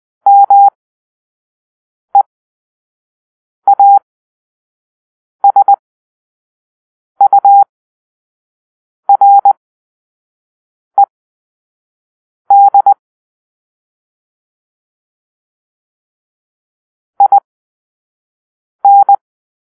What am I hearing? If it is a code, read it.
Morse code, heard as a keyed tone, about 20 wpm, spaced as for 5 wpm, 800 Hz. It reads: MEASURED IN
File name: arrl / Grp-5 Drill-4